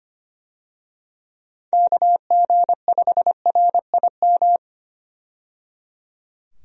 Morse code, heard as a keyed tone, about 25 wpm, 700 Hz. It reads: KG5RIM
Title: USA Call Signs - 6 Characters - 25wpm